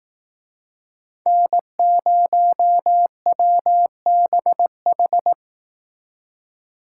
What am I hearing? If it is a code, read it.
Morse code, heard as a keyed tone, about 18 wpm, 700 Hz. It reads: N0WBH
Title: Speed Racing - USA 1x3 Call Signs 15wpm